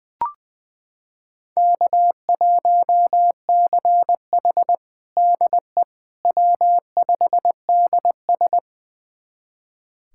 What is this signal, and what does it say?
Morse code, heard as a keyed tone, about 20 wpm, 700 Hz. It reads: K1CH DE W5DS
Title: USA Callsign DE Callsign - 4 Characters - 20wpm